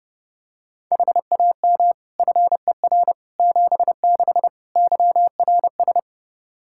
Morse code, heard as a keyed tone, about 29 wpm, 700 Hz. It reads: HAM FER 76 YRS
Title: Practice QSO - Exchange 3 - More Info - 30wpm